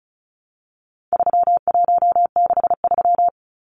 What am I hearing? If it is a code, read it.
Morse code, heard as a keyed tone, about 34 wpm, 700 Hz. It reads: 3163
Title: Numbers - 4-Digit 35wpm